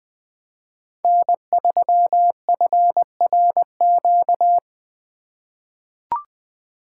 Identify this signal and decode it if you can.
Morse code, heard as a keyed tone, about 20 wpm, 700 Hz. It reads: N3FRQ
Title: USA Call Signs - 4x Character Spacing - 20wpm